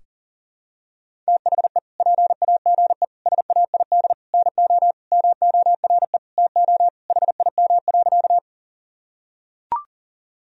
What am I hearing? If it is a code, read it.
Morse code, heard as a keyed tone, about 40 wpm, 700 Hz. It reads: THE PAGE SAID NO MORE TO HIM.
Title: Sentences from Top 1000 Words - Review Part 1 - 40wpm